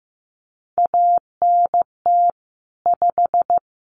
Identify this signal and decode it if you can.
Morse code, heard as a keyed tone, about 14 wpm, 700 Hz. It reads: ANT 5
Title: Common QSO Phrases 15wpm